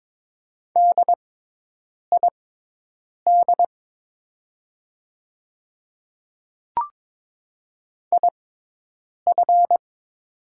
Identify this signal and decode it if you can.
Morse code, heard as a keyed tone, about 22 wpm, spaced as for 8 wpm, 700 Hz. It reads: DID IF
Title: 6x Character Spacing - Top 200 Words - 22wpm